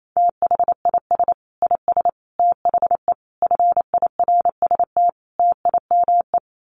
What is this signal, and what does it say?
Morse code, heard as a keyed tone, about 28 wpm, 700 Hz. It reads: THIS IS THE FIRST TIME
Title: Sets of 5 Words 28wpm